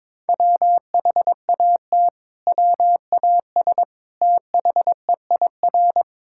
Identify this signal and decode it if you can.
Morse code, heard as a keyed tone, about 22 wpm, 700 Hz. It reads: WHAT WAS THEIR
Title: Sentences from Top 1000 Words - Review Part 6 - 22wpm